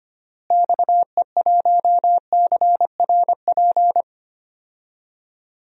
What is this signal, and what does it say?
Morse code, heard as a keyed tone, about 25 wpm, 700 Hz. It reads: XE1CRP
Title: DXCC-100 Call Signs - 6 Characters - Repeated 3x - 25wpm